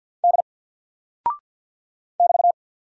Morse code, heard as a keyed tone, about 48 wpm, 700 Hz. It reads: D <BT>
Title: Single Character (Letter:Number:Punctuation:Prosign) Alt 50wpm